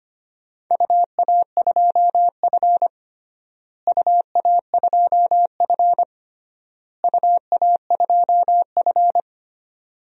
Morse code, heard as a keyed tone, about 25 wpm, 700 Hz. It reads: UA2F UA2F UA2F
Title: DXCC-100 Call Signs - 4 Characters - Repeated 3x - 25wpm